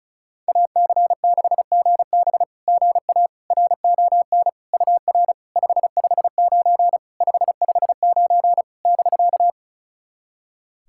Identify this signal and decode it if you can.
Morse code, heard as a keyed tone, about 34 wpm, 700 Hz. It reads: AC6GB GA RON UR 559 559 <BK>
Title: SOTA QSO Practice 35wpm